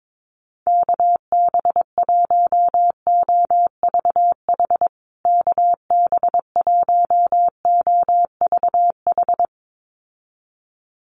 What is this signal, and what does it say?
Morse code, heard as a keyed tone, about 22 wpm, 700 Hz. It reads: KB1OVH KB1OVH
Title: USA Call Signs Repeated 2x 22wpm